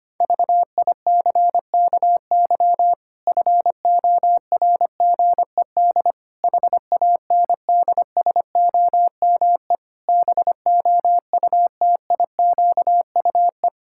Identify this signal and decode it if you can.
Morse code, heard as a keyed tone, about 25 wpm, 700 Hz. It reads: VICKY FORGED HANDSOME BOUTIQUE
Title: Pangram Challenge v2 - 25wpm